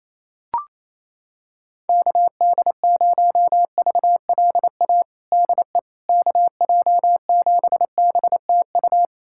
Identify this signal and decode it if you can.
Morse code, heard as a keyed tone, about 28 wpm, 700 Hz. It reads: KD0VLA DE KJ7BTU
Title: USA Callsign DE Callsign - 6 Characters - 28wpm